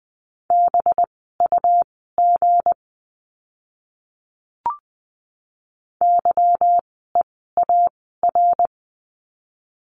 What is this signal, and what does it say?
Morse code, heard as a keyed tone, about 20 wpm, spaced as for 14 wpm, 700 Hz. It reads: BUG YEAR
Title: 185 Lesson 30f - Letter b - Words 20wpm